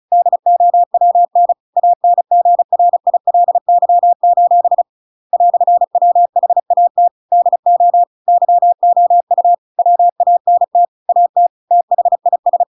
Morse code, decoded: DOWN ANGRILY: "WHAT DO YOU WANT AT THIS

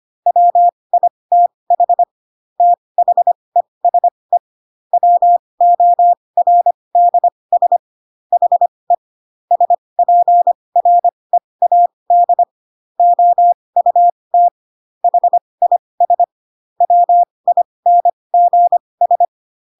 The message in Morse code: WITH THESE WORDS HE SPREAD OUT HIS WINGS

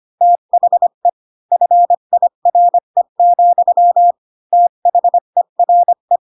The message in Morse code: THE FIRE, THERE

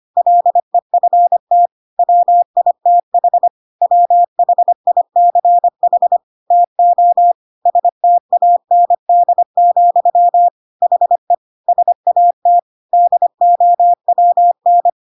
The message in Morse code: LEFT WITH WHICH TO STAND, HE SAT DOWN

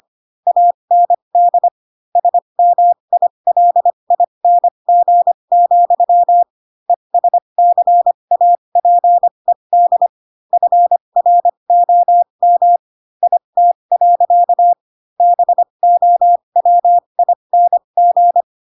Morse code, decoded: AND SMILING, ESCAPED FROM IT. BOWING

